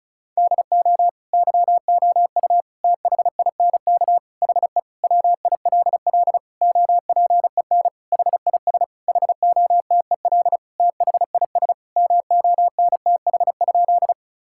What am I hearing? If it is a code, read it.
Morse code, heard as a keyed tone, about 34 wpm, 700 Hz. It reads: DO YOU THINK HE WILL OPEN HIS HOTEL THIS MONTH?
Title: Sentences from Top 1600 Words 35wpm